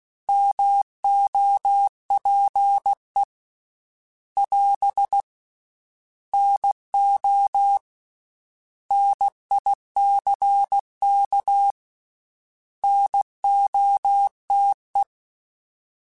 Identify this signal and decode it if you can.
Morse code, heard as a keyed tone, about 16 wpm, 800 Hz. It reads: MOPE <AS> NO NICK NOTE
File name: Misc / Misc-mp3s / Misc @15